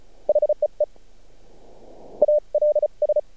Morse code, heard as a keyed tone, about 34 wpm, 600 Hz. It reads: HEE ALS
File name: archived / CRSnet